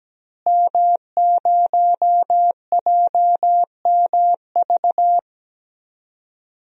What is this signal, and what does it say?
Morse code, heard as a keyed tone, about 17 wpm, 700 Hz. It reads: M0JMV
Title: DXCC-100 Call Signs - 5 Characters - Repeated 3x - 17wpm